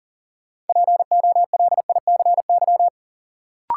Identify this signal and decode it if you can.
Morse code, heard as a keyed tone, about 40 wpm, 700 Hz. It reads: POLICY
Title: Sentences from Top 1000 Words - Review Part 5 - New Word Drill - 40wpm